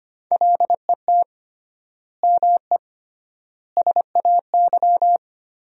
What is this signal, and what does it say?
Morse code, heard as a keyed tone, about 25 wpm, 700 Hz. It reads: LET ME SAY
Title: 3x Word-Spacing - Sentences from Top 300 Words - 25wpm